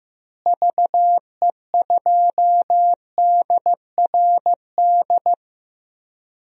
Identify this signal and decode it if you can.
Morse code, heard as a keyed tone, about 15 wpm, 700 Hz. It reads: VE2DRD
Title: DXCC-100 Call Signs - 6 Characters - Repeated 3x - 15wpm